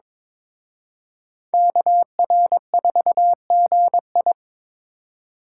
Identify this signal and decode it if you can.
Morse code, heard as a keyed tone, about 26 wpm, 700 Hz. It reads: KR4GI
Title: Speed Racing - USA 2x2 Call Signs 22wpm